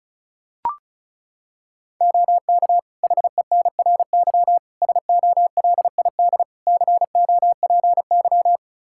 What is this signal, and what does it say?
Morse code, heard as a keyed tone, about 34 wpm, 700 Hz. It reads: OK HENRY SOLID COPY
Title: Practice QSO - Exchange 3 - More Info - 35wpm